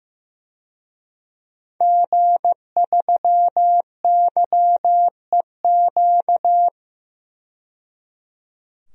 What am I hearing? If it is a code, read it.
Morse code, heard as a keyed tone, about 15 wpm, 700 Hz. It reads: G3YEQ
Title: DXCC-100 Call Signs - 5 Characters - 15wpm